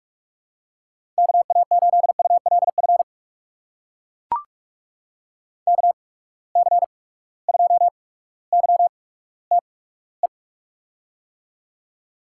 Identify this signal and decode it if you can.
Morse code, heard as a keyed tone, about 44 wpm, spaced as for 19 wpm, 700 Hz. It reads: KA8ULF KC2YTE
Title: USA Call Signs - 6 Characters - 8x Character Spacing - 45wpm